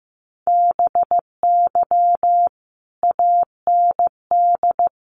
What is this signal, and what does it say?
Morse code, heard as a keyed tone, about 15 wpm, 700 Hz. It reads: BY AND
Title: Sentences from Top 300 Words 15wpm